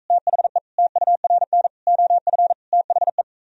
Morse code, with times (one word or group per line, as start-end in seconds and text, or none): 0.1-0.6 THE
0.8-1.7 TURN
1.9-2.5 OF
2.7-3.2 THE